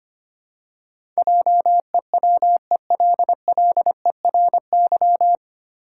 Morse code, heard as a keyed tone, about 25 wpm, 700 Hz. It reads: JEWELLERY
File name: Nine Letter Words - 25wpm